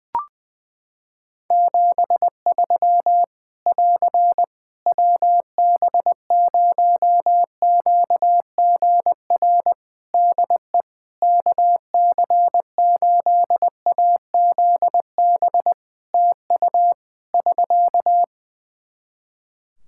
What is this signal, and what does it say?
Morse code, heard as a keyed tone, about 20 wpm, 700 Hz. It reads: 73 <AR> WB0QGR DE KC8AZB TU <SK>
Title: Practice QSO - Exchange 4 - Ending - 20wpm